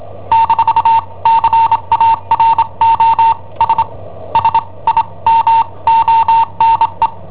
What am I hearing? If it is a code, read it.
Morse code, heard as a keyed tone, about 23 wpm, spaced as for 20 wpm, 910 Hz. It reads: - CAROS SIMONE
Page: Cassetta esercizi preparata da I7OHP per IW5ELP